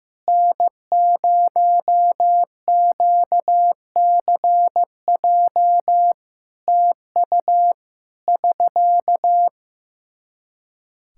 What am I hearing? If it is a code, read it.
Morse code, heard as a keyed tone, about 16 wpm, 700 Hz. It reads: N0QCJ TU <SK>
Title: Practice QSO - Exchange 4 - Ending - 15wpm